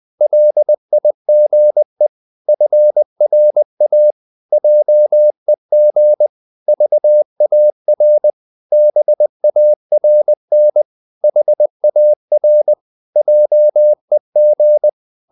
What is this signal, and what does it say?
Morse code, heard as a keyed tone, about 20 wpm, 600 Hz. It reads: LIGE FRA JEG VAR BARN HAR JEG
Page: Ferie 20 wpm